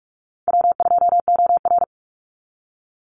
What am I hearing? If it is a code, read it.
Morse code, heard as a keyed tone, about 44 wpm, 700 Hz. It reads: W2OR
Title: Easy Call Signs Repeated 3x - 45wpm